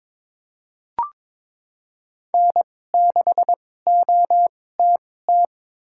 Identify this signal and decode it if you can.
Morse code, heard as a keyed tone, about 22 wpm, spaced as for 16 wpm, 700 Hz. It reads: N6OTT
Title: USA Call Signs - 5 Characters - 2x Character Spacing - 22wpm